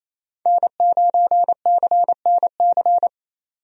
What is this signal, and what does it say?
Morse code, heard as a keyed tone, about 28 wpm, 700 Hz. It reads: N9CNC
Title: USA Call Signs - 5 Characters - 28wpm